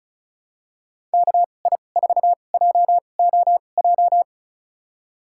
Morse code, heard as a keyed tone, about 34 wpm, spaced as for 25 wpm, 690 Hz. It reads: KI4JOJ
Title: USA Call Signs - 6 Characters - 2x Character Spacing - 35wpm